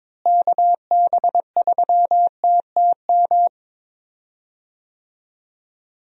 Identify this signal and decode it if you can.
Morse code, heard as a keyed tone, about 22 wpm, 700 Hz. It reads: KB3TTM
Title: USA Call Signs - 6 Characters - Repeated 3x - 22wpm